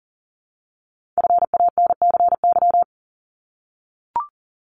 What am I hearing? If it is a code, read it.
Morse code, heard as a keyed tone, about 41 wpm, 700 Hz. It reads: FANCY
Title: Multi-letter Phonemes 40wpm